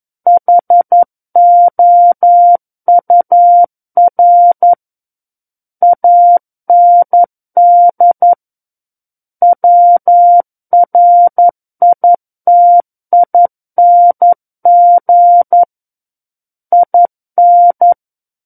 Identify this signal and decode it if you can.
Morse code, heard as a keyed tone, about 11 wpm, 700 Hz. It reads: HOUR AND WRITING IN
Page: War of the Worlds - 02-Chapter 2 - 11 WPM